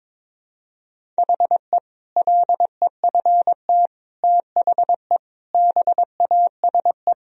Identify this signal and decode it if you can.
Morse code, heard as a keyed tone, about 22 wpm, 700 Hz. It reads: HE LEFT THE BASE
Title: Sentences from Top 1000 Words - Review Part 1 - 22wpm